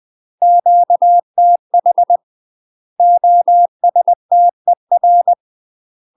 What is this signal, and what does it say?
Morse code, heard as a keyed tone, about 20 wpm, 700 Hz. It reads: QTH OSTER
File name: UK MASTER QSOs TRAINING / 20WPM / UK MASTER QSOs 0003